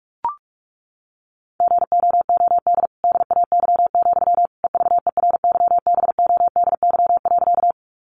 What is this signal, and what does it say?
Morse code, heard as a keyed tone, about 44 wpm, 700 Hz. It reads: GOOD DAY, EVERYBODY.
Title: Sentences from Top 1400 Words 45wpm